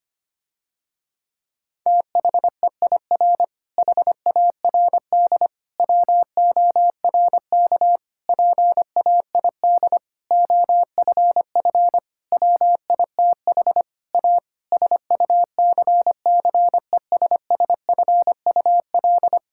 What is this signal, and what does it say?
Morse code, heard as a keyed tone, about 25 wpm, 700 Hz. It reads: THEIR HARD WORK PAID OFF WITH A SUCCESSFUL
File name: practice / 25wpm / Difficult Sentences with Hints - 25wpm